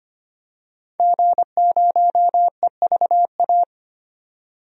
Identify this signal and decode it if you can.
Morse code, heard as a keyed tone, about 25 wpm, 700 Hz. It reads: G0EVA
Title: DXCC-100 Call Signs - 5 Characters - 25wpm